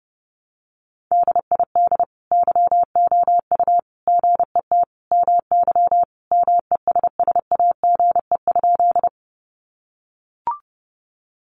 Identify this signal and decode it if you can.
Morse code, heard as a keyed tone, about 30 wpm, 700 Hz. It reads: DID YOU GET MY MESSAGE?
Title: Sentences from Top 1100 Words 30wpm